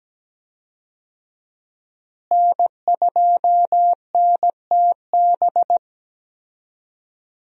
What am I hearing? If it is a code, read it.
Morse code, heard as a keyed tone, about 17 wpm, 700 Hz. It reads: N2NTB
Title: USA Call Signs - 5 Characters - Repeated 3x - 17wpm